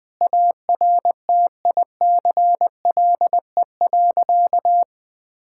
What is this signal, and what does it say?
Morse code, heard as a keyed tone, about 20 wpm, 700 Hz. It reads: ARTICLE.
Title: Sentences with 4-Word Phrases - 20wpm